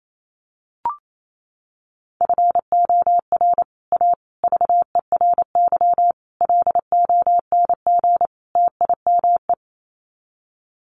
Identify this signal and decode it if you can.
Morse code, heard as a keyed tone, about 28 wpm, 700 Hz. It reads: FOR A VERY LONG TIME
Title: Sets of 5 Words 28wpm